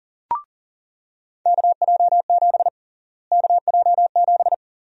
Morse code, heard as a keyed tone, about 40 wpm, 700 Hz. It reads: KJ7 KJ7
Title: USA Call Sign Prefixes - Repeated 2x - 40wpm